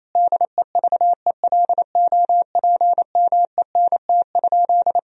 Morse code, heard as a keyed tone, about 28 wpm, 700 Hz. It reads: DEVELOPMENT?
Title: Sentences from Top 1000 Words - Review Part 4 - 28wpm